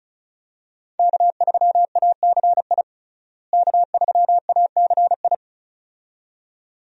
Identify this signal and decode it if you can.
Morse code, heard as a keyed tone, about 36 wpm, 700 Hz. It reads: K3ACI K3ACI
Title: USA Call Signs - 5 Characters - Repeated 2x - 35wpm